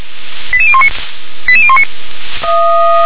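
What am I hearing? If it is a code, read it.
Five-tone selective call (ZVEI), heard as quick runs of five tones, tones of about 70 ms. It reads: 88019 88019
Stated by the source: Five-tone ZVEI selective call 88163 received by SDR.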